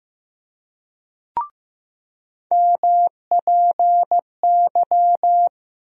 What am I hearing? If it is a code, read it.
Morse code, heard as a keyed tone, about 15 wpm, 700 Hz. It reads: MPY
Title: USA Call Sign Postfixes - 3 Letters 15wpm